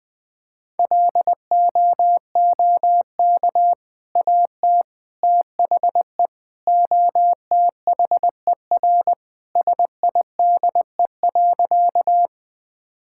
Morse code, heard as a keyed tone, about 20 wpm, 700 Hz. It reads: LOOK AT THE OTHER SIDE.
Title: Sentences with 2-Word Phrases - 20wpm